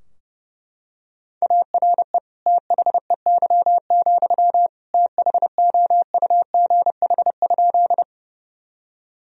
Morse code, heard as a keyed tone, about 32 wpm, 700 Hz. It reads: ARE THEY, THOUGH?
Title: Sentences from Top 1000 Words - Review Part 2 - 30wpm